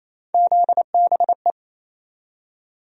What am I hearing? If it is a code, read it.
Morse code, heard as a keyed tone, about 27 wpm, 700 Hz. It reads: ZBE
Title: USA Call Sign Postfixes - 3 Letters 28wpm